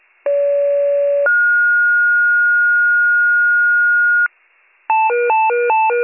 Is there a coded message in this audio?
Yes: two-tone page from 0.3 to 4.3 s, heard as a one-second tone, then a three-second tone, about 570 then 1430 Hz.